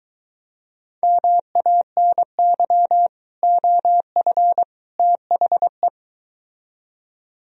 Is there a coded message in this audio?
Yes: Morse code from 1.0 to 5.9 s, heard as a keyed tone, about 23 wpm, 700 Hz. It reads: MANY OF THE